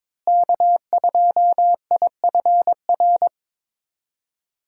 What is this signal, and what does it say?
Morse code, heard as a keyed tone, about 21 wpm, 700 Hz. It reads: K2IFR
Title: USA Call Signs - 5 Characters - Repeated 3x - 22wpm